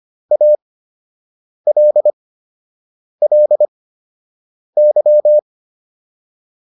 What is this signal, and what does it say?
Morse code, heard as a keyed tone, about 25 wpm, spaced as for 7 wpm, 600 Hz. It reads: ALLY
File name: LY4 07